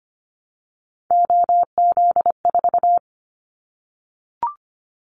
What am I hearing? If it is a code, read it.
Morse code, heard as a keyed tone, about 25 wpm, 700 Hz. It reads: OZ4
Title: Callsign Prefixes for Top 50 DXCC - 25wpm